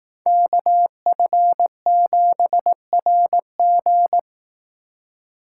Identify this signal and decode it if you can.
Morse code, heard as a keyed tone, about 18 wpm, 700 Hz. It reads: KF7RG
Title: Speed Racing - USA 2x2 Call Signs 15wpm